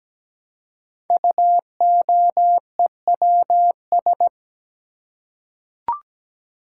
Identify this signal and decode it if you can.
Morse code, heard as a keyed tone, about 17 wpm, 700 Hz. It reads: UOEWS
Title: Code Groups - 5 Letters - 17wpm